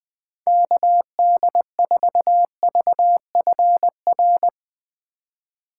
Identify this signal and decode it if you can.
Morse code, heard as a keyed tone, about 19 wpm, 700 Hz. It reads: KD4VFR
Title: Speed Racing - USA 2x3 Call Signs 17wpm